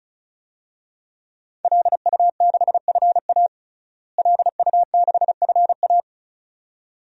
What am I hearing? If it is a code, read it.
Morse code, heard as a keyed tone, about 36 wpm, 700 Hz. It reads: LU6FA LU6FA
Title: DXCC-100 Call Signs - 5 Characters - Repeated 2x - 35wpm